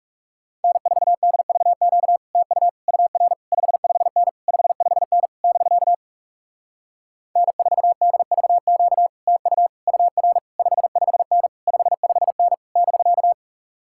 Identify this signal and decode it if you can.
Morse code, heard as a keyed tone, about 40 wpm, 700 Hz. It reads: N4DVQ TU UR 55N 55N <BK> N4DVQ TU UR 55N 55N <BK>